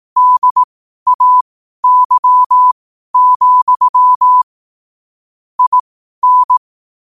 Morse code, heard as a keyed tone, about 18 wpm, spaced as for 12 wpm, 1000 Hz. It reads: DAY, IN
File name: qotdfiles / 12wpm